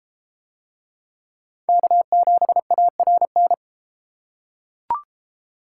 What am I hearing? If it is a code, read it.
Morse code, heard as a keyed tone, about 31 wpm, 700 Hz. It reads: K7ARN